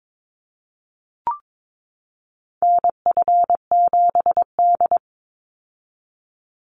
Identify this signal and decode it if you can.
Morse code, heard as a keyed tone, about 22 wpm, 700 Hz. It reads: NF7D
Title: Call Signs - 22wpm